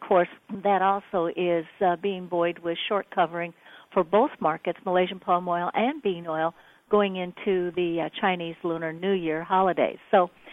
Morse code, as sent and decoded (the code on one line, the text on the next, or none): none